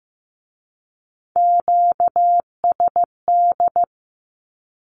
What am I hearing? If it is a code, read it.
Morse code, heard as a keyed tone, about 15 wpm, 700 Hz. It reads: QSD
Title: Q-Codes with Spoken Meaning - 15wpm